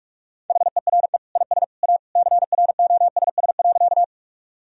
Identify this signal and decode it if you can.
Morse code, heard as a keyed tone, about 46 wpm, 700 Hz. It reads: HERE IS A CROSS.
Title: Sentences from Top 1000 Words - Review Part 1 - 45wpm